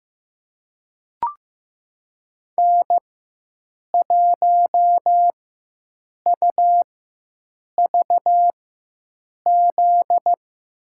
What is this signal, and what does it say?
Morse code, heard as a keyed tone, about 15 wpm, spaced as for 7 wpm, 700 Hz. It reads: N1UVZ
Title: USA Call Signs - 5 Characters - 4x Character Spacing - 15wpm